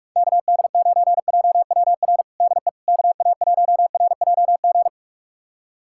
Morse code, heard as a keyed tone, about 42 wpm, 700 Hz. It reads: KD9JWR DE KA1RJG
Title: USA Callsign DE Callsign 45wpm